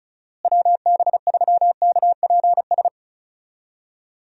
Morse code, heard as a keyed tone, about 34 wpm, 700 Hz. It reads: WB3KPS
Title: USA Call Signs - 6 Characters - Repeated 2x - 35wpm